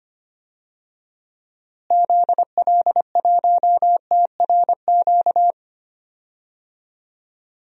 Morse code, heard as a keyed tone, about 25 wpm, 700 Hz. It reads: ZL1TRQ
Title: DXCC-100 Call Signs - 6 Characters - Repeated 2x - 25wpm